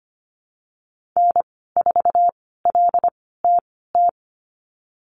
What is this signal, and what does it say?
Morse code, heard as a keyed tone, about 25 wpm, spaced as for 16 wpm, 700 Hz. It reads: N4LTT
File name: Lesson 15g - Number 5 - Call Signs 25wpm